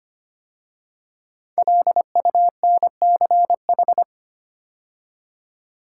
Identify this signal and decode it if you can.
Morse code, heard as a keyed tone, about 25 wpm, 700 Hz. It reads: LUNCH
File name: 5 Letter Words - First 3 Letters Given - 25wpm